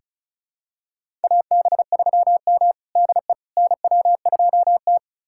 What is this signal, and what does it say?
Morse code, heard as a keyed tone, about 36 wpm, 700 Hz. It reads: AB3M DE NW2T
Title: USA Callsign DE Callsign - 4 Characters - 35wpm